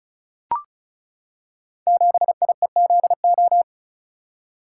Morse code, heard as a keyed tone, about 36 wpm, 700 Hz. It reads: 7IEZO
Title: Code Groups - 5 Letter:Numbers - 35wpm